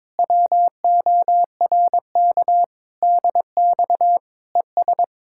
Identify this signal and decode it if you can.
Morse code, heard as a keyed tone, about 22 wpm, 700 Hz. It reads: WORK DX ES
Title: Speed-Racing Common QSO Phrases - 22wpm